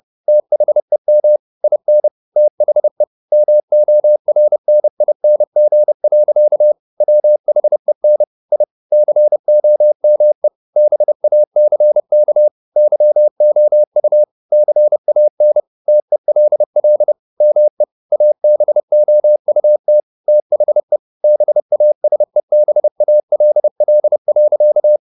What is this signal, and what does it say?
Morse code, heard as a keyed tone, about 30 wpm, 600 Hz. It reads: THEM IN THE MORNING. WHEN I COME BACK YOU CAN TELL ME ABOUT THE BASEBALL.
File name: the old man and the sea / the old man and the sea 1 30wpm